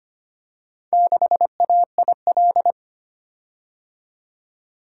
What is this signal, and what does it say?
Morse code, heard as a keyed tone, about 25 wpm, 700 Hz. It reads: 6AIL
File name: practice / 25wpm / Four Letter-Number Combinations - 25wpm